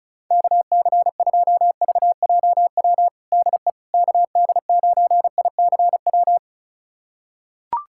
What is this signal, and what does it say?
Morse code, heard as a keyed tone, about 34 wpm, 690 Hz. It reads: KC2VJW DE KD9ICW
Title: USA Callsign DE Callsign 35wpm